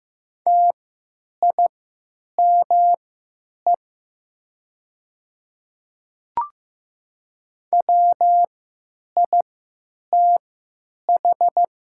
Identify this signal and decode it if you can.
Morse code, heard as a keyed tone, about 15 wpm, spaced as for 9 wpm, 700 Hz. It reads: TIME WITH
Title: 3x Character Spacing - Top 100 Words - 15wpm